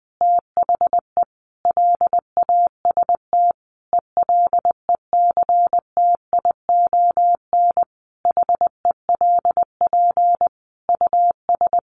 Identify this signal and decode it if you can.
Morse code, heard as a keyed tone, about 20 wpm, 700 Hz. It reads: THE LAST ELECTION HELP US
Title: Sentences from Top 1500 Words 20wpm